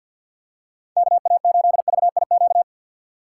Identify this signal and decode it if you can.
Morse code, heard as a keyed tone, about 46 wpm, 700 Hz. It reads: KA8VIQ
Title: USA Call Signs - 6 Characters - 3x Character Spacing - 50wpm